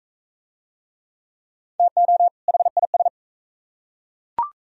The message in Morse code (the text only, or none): TO HIS